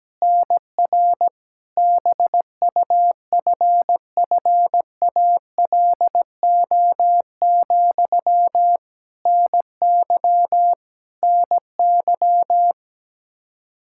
Morse code, heard as a keyed tone, about 17 wpm, 700 Hz. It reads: NR BUFFALO, NY NY
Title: K1USN SST QSO Practice 17wpm